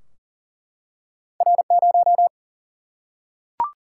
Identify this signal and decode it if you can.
Morse code, heard as a keyed tone, about 40 wpm, 700 Hz. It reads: R 0
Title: Single-Letter Single-Number 40wpm